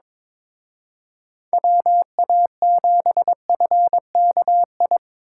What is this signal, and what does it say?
Morse code, heard as a keyed tone, about 25 wpm, 700 Hz. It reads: WA7FKI
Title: Speed Racing - USA 2x3 Call Signs 22wpm